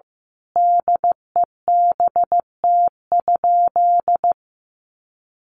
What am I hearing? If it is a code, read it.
Morse code, heard as a keyed tone, about 15 wpm, 700 Hz. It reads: DEBT?
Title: Sentences from Top 1300 Words 15wpm